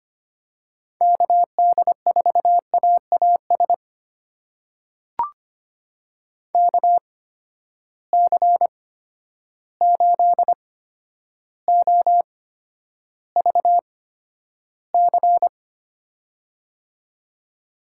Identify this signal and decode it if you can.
Morse code, heard as a keyed tone, about 25 wpm, spaced as for 11 wpm, 700 Hz. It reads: KD4AAS KC8OVC
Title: USA Call Signs - 6 Characters - 8x Character Spacing - 25wpm